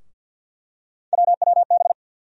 Morse code, heard as a keyed tone, about 47 wpm, 700 Hz. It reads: WWB
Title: USA Call Sign Postfixes - 3 Letters 50wpm